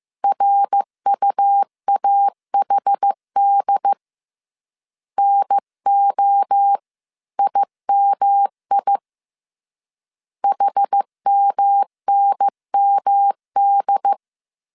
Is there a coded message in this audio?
Yes: Morse code from 0.2 to 14.1 s, heard as a keyed tone, about 15 wpm, 800 Hz. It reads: RUAHD NOIMI HMNMD